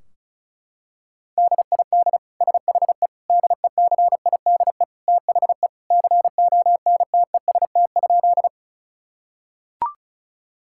Morse code, heard as a keyed tone, about 36 wpm, 700 Hz. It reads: DID SHE DECIDE THE CONTEST?
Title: Sentences from Top 1600 Words 35wpm